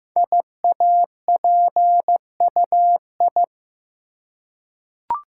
Morse code, heard as a keyed tone, about 15 wpm, 700 Hz. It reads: IAPUI
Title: Code Groups - 5 Letters - 15wpm